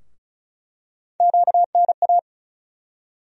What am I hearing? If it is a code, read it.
Morse code, heard as a keyed tone, about 36 wpm, 700 Hz. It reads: QNA
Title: USA Call Sign Postfixes - 3 Letters 35wpm